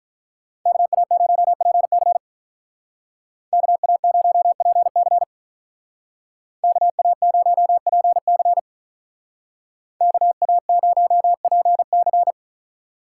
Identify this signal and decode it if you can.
Morse code, heard as a keyed tone, about 43 wpm, 700 Hz. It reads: KA0PC KA0PC KA0PC KA0PC
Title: Speed Racing - USA 2x2 Call Signs 35wpm